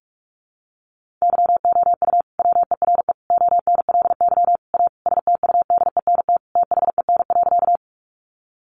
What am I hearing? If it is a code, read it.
Morse code, heard as a keyed tone, about 44 wpm, 700 Hz. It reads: YOU WERE ONLY A STUDENT THEN.
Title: Sentences from Top 1000 Words 45wpm